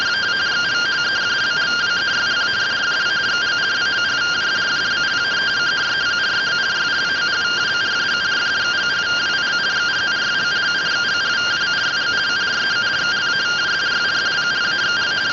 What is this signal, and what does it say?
Two-tone teleprinter signal, about 170 Hz shift.